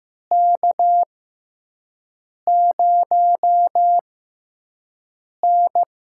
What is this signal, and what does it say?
Morse code, heard as a keyed tone, about 16 wpm, spaced as for 5 wpm, 700 Hz. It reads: K0N
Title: USA Call Signs - 6x Character Spacing - 15wpm